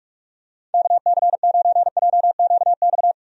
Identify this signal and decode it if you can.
Morse code, heard as a keyed tone, about 42 wpm, 700 Hz. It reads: KC9JQX
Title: USA Call Signs - 6 Characters - Repeated 3x - 45wpm